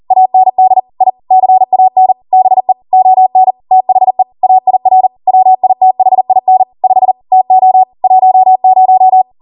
Morse code, heard as a keyed tone, about 40 wpm, 750 Hz. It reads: AND I CAN BE ON THE AIR WITHIN 5 TO 10